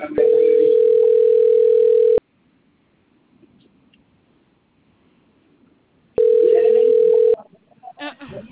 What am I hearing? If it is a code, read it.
Ringback tone (North America), 440 + 480 Hz.